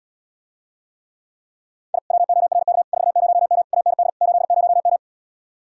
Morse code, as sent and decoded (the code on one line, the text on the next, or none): .. / .-- --- .-. -.- / ....- ----- -- / .- -. -.. / ---.. ----- --
I WORK 40M AND 80M